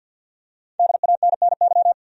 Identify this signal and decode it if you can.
Morse code, heard as a keyed tone, about 47 wpm, 700 Hz. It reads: DANNY
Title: USA Common Names - 50wpm